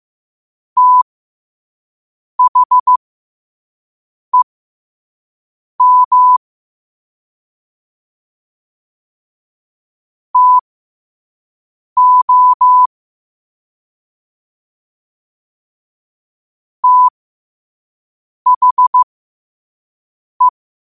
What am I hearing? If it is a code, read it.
Morse code, heard as a keyed tone, about 15 wpm, spaced as for 5 wpm, 1000 Hz. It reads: THEM TO THE
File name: qotdfiles / 5wpm